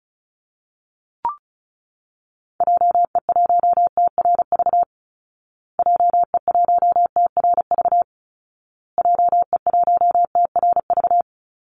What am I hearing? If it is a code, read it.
Morse code, heard as a keyed tone, about 34 wpm, 700 Hz. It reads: JE1TRV JE1TRV JE1TRV
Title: Call Signs Repeated 2x - 35wpm